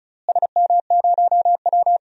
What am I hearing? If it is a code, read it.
Morse code, heard as a keyed tone, about 34 wpm, 700 Hz. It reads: SM0W
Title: DXCC-100 Call Signs - 4 Characters - Repeated 3x - 35wpm